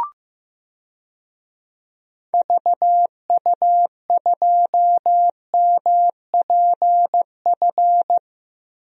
Morse code, heard as a keyed tone, about 15 wpm, 700 Hz. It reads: VU2MPF
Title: DXCC-100 Call Signs - 6 Characters - Repeated 2x - 15wpm